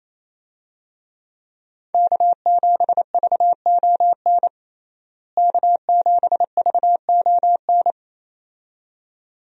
Morse code, heard as a keyed tone, about 27 wpm, 700 Hz. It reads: K7VON K7VON
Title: USA Call Signs - 5 Characters - Repeated 2x - 28wpm